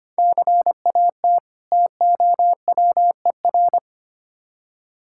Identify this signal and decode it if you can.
Morse code, heard as a keyed tone, about 25 wpm, 700 Hz. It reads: CAT TOWER
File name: practice / 25wpm / Difficult Words with Hints - 25wpm